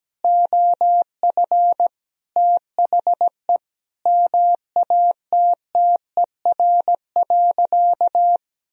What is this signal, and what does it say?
Morse code, heard as a keyed tone, about 17 wpm, 700 Hz. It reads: OF THE MATTER.
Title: Sentences from Top 1000 Words - Review Part 3 - 17wpm